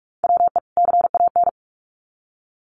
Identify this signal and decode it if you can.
Morse code, heard as a keyed tone, about 44 wpm, 700 Hz. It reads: WE CAN
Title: Sets of 2 Words 45wpm